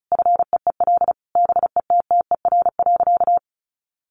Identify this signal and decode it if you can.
Morse code, heard as a keyed tone, about 34 wpm, 700 Hz. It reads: FEEL BETTER.
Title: Easy Phrases 35wpm